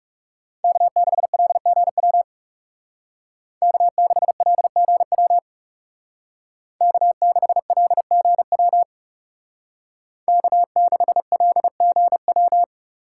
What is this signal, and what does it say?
Morse code, heard as a keyed tone, about 35 wpm, 700 Hz. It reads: K6LGW K6LGW K6LGW K6LGW
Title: Speed Racing - USA Call Signs 30wpm